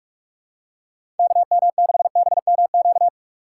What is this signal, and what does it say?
Morse code, heard as a keyed tone, about 47 wpm, 700 Hz. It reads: KM6BMQ
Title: USA Call Signs - 6 Characters - 4x Character Spacing - 45wpm